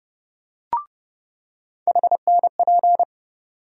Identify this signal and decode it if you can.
Morse code, heard as a keyed tone, about 30 wpm, 700 Hz. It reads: HNP
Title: USA Call Sign Postfixes - 3 Letters 30wpm